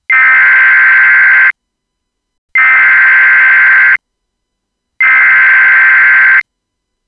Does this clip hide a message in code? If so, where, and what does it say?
Yes: Emergency Alert System (SAME) tones (header bursts) from 0.1 to 6.4 s, heard as harsh buzzing bursts.